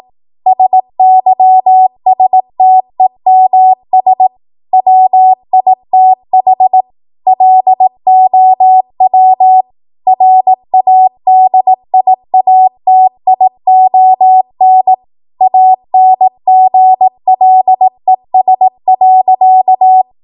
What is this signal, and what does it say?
Morse code, heard as a keyed tone, about 18 wpm, 750 Hz. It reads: SYSTEMS WITH LOW RADIATION ANGLES.
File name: Morse / 18WPM